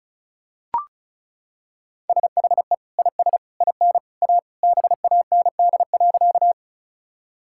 Morse code, heard as a keyed tone, about 36 wpm, 700 Hz. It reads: SHE IS IN A BAND.
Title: Sentences from Top 1600 Words 35wpm